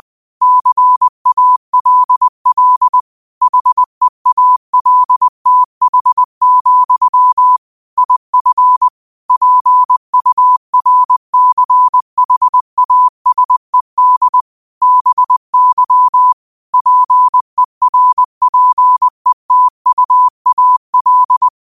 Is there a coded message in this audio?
Yes: Morse code from 0.4 to 21.5 s, heard as a keyed tone, about 20 wpm, 1000 Hz. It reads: CALL HEALTH, IF PURCHASED BY PERPETUAL